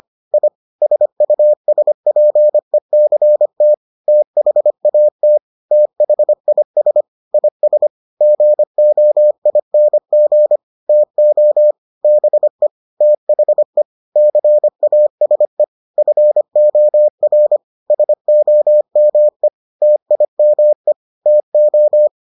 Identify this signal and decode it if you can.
Morse code, heard as a keyed tone, about 25 wpm, 600 Hz. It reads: I SUSPECT THAT THIS IS GOING TO BE THE CASE FOR SOME TIME TO